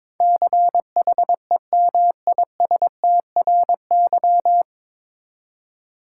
Morse code, heard as a keyed tone, about 22 wpm, 700 Hz. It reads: CHEMISTRY
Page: Morse Code Ninja practice 22wpm